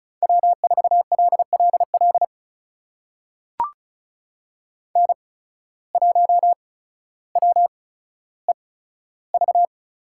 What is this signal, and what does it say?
Morse code, heard as a keyed tone, about 36 wpm, spaced as for 15 wpm, 700 Hz. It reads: W4LLL N1WEV
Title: USA Call Signs - 5 Characters - 8x Character Spacing - 35wpm